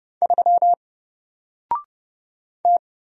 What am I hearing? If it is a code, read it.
Morse code, heard as a keyed tone, about 30 wpm, 700 Hz. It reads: 3 T
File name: Single-Letter-Number-30wpm